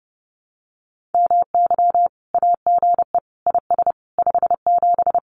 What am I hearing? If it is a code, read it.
Morse code, heard as a keyed tone, about 30 wpm, 700 Hz. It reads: MY AGE IS 57
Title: Common QSO Phrases Encore 30wpm